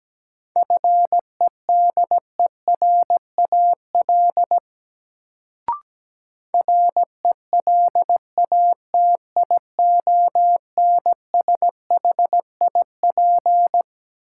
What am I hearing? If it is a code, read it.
Morse code, heard as a keyed tone, about 17 wpm, 700 Hz. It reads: FEDERAL RELATIONSHIP
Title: Sentences from Top 1100 Words - New Word Drill Alt - 17wpm